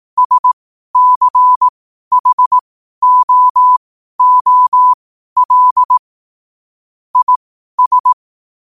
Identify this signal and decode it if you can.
Morse code, heard as a keyed tone, about 18 wpm, spaced as for 12 wpm, 1000 Hz. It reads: SCHOOL IS
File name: qotdfiles / 12wpm